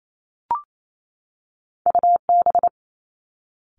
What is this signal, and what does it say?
Morse code, heard as a keyed tone, about 28 wpm, 700 Hz. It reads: U B
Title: Two Challenging Letter-Number Combinations 28wpm